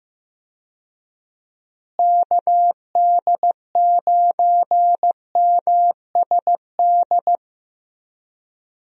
Morse code, heard as a keyed tone, about 16 wpm, 700 Hz. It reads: KD9MSD
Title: USA Call Signs - 6 Characters - Repeated 2x - 15wpm